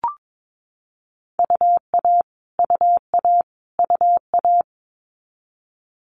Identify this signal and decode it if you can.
Morse code, heard as a keyed tone, about 22 wpm, 700 Hz. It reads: UA UA UA
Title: Multi-letter Phonemes Repeated 3x 22wpm